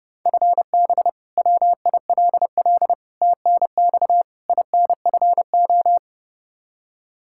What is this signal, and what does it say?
Morse code, heard as a keyed tone, about 30 wpm, 700 Hz. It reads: FB WILL TNX INFO
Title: Practice QSO - Exchange 3 - More Info - 30wpm